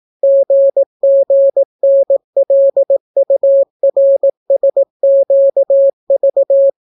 Morse code, heard as a keyed tone, about 18 wpm, 550 Hz. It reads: GGNLURSQV